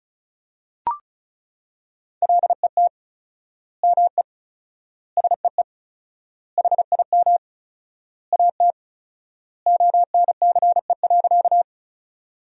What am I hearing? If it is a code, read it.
Morse code, heard as a keyed tone, about 34 wpm, 700 Hz. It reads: LET ME SEE HIM AT ONCE.
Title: 4x Word-Spacing - Sentences from Top 300 Words - 35wpm